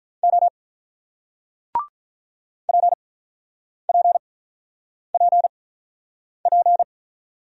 Morse code, heard as a keyed tone, about 48 wpm, spaced as for 9 wpm, 700 Hz. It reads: O PPPP